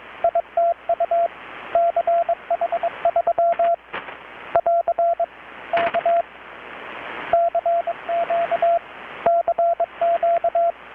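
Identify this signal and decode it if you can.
Morse code, heard as a keyed tone, about 21 wpm, 680 Hz, with radio noise. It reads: ITU CH3 <AR> K CQ CQ
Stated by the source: Morse from KPH's Night of Nights XX maritime radiotelegraphy event, received in Chiapas, Mexico about 3,800 km away on a Kenwood R-600 with a grounded 10 m long-wire antenna.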